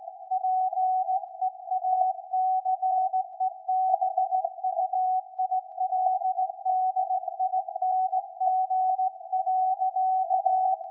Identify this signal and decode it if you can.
Morse code, heard as a keyed tone, about 17 wpm, 750 Hz. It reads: PEACEBUILDING.
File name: iz7slz / VLF / SAQ 13-02-2023